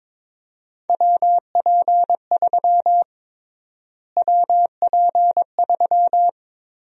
Morse code, heard as a keyed tone, about 23 wpm, 700 Hz. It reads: WP3 WP3
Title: USA Call Sign Prefixes - Repeated 2x - 22wpm